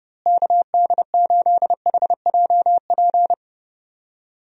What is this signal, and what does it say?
Morse code, heard as a keyed tone, about 30 wpm, 700 Hz. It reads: KD8HJP
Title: USA Call Signs - 6 Characters - Repeated 2x - 30wpm